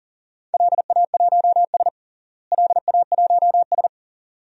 Morse code, heard as a keyed tone, about 37 wpm, 700 Hz. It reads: LA1S LA1S
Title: DXCC-100 Call Signs - 4 Characters - Repeated 2x - 40wpm